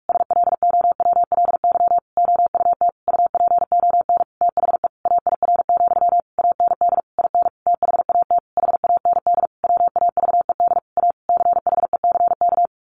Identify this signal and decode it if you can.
Morse code, heard as a keyed tone, about 44 wpm, 700 Hz. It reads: SLOWLY OUT UPON THE AIR, AND IN THAT HAND WAVED A CHECK
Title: A Sound Of Thunder 45wpm